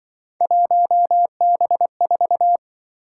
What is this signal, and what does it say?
Morse code, heard as a keyed tone, about 24 wpm, 700 Hz. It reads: 1B4
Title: Speed-Racing - Three Letter-Number Combinations - 20wpm